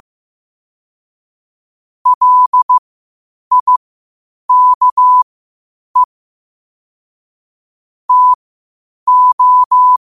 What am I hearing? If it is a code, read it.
Morse code, heard as a keyed tone, about 15 wpm, spaced as for 8 wpm, 1000 Hz. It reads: LIKE TO
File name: qotdfiles / 8wpm